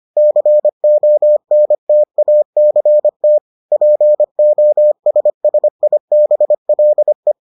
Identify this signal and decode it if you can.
Morse code, heard as a keyed tone, about 25 wpm, 600 Hz. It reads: CONTACT POSSIBLE